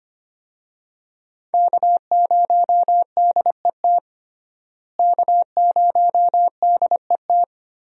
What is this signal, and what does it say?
Morse code, heard as a keyed tone, about 25 wpm, 700 Hz. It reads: K0DET K0DET
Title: USA Call Signs - 5 Characters - Repeated 3x - 25wpm